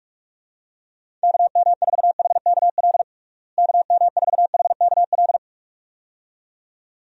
Morse code, heard as a keyed tone, about 46 wpm, 700 Hz. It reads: KM4HKL KM4HKL
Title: USA Call Signs - 6 Characters - Repeated 2x - 45wpm